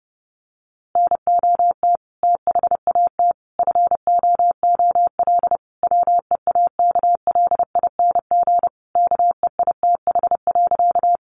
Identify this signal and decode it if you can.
Morse code, heard as a keyed tone, about 30 wpm, 700 Hz. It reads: NOT THAT FOOL WEAKLING KEITH.
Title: A Sound Of Thunder 30wpm